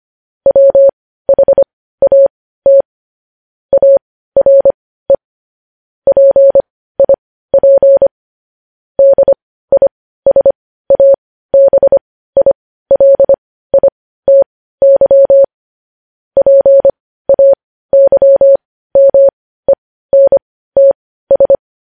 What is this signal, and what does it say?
Morse code, heard as a keyed tone, about 25 wpm, spaced as for 15 wpm, 550 Hz. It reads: WHAT ARE PIP DISABILITY PAYMENTS
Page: news headlines in morse code at 15 wpm